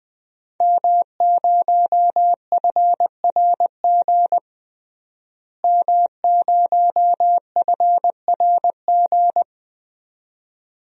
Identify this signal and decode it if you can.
Morse code, heard as a keyed tone, about 21 wpm, 700 Hz. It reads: M0FRG M0FRG
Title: DXCC-100 Call Signs - 5 Characters - Repeated 2x - 20wpm